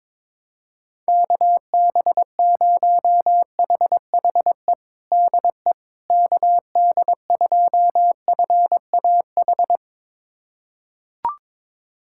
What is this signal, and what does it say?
Morse code, heard as a keyed tone, about 22 wpm, 700 Hz. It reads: KB0HHE DE KD2FAH
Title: USA Callsign DE Callsign 22wpm